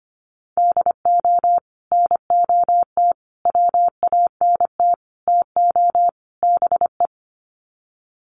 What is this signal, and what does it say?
Morse code, heard as a keyed tone, about 25 wpm, 700 Hz. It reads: DO NOT WANT TO BE
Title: Sets of 5 Words 25wpm